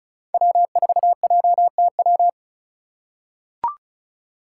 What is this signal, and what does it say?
Morse code, heard as a keyed tone, about 34 wpm, 700 Hz. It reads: W4JTW
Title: USA Call Signs - 2x Character Spacing - 35wpm